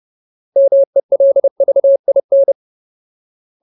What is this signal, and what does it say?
Morse code, heard as a keyed tone, about 30 wpm, 550 Hz. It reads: MELVIN
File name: Ultimate Word List / 30WPM Ultimate